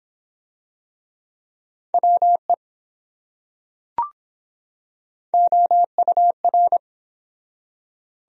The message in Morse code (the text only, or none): WE OUR